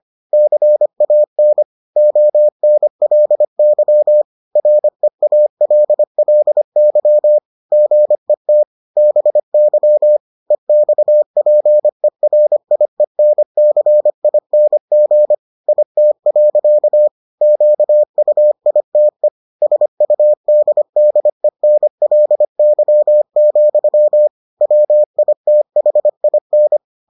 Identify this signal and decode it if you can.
Morse code, heard as a keyed tone, about 25 wpm, 600 Hz. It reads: CAN ONLY REALLY GET BY EXPERIENCING IT. QUITE SUDDENLY, WITHIN